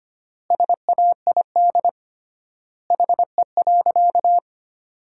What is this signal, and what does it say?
Morse code, heard as a keyed tone, about 25 wpm, 700 Hz. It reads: SAID HE.
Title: 3x Word-Spacing - Sentences from Top 100 Words - 25wpm